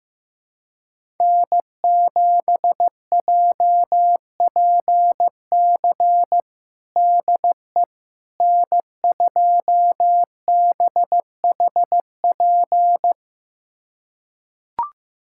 Morse code, heard as a keyed tone, about 16 wpm, 700 Hz. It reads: N7JPC DE N2BHP
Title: USA Callsign DE Callsign - 5 Characters - 15wpm